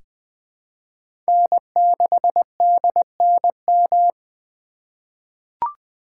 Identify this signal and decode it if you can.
Morse code, heard as a keyed tone, about 20 wpm, 700 Hz. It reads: N6DNM
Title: 2023 SOTA Chaser Call Signs - 20wpm